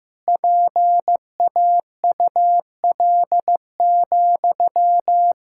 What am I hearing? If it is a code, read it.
Morse code, heard as a keyed tone, about 15 wpm, 700 Hz. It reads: PAUL,
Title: Practice QSO - Exchange 1 - Beginning - 15wpm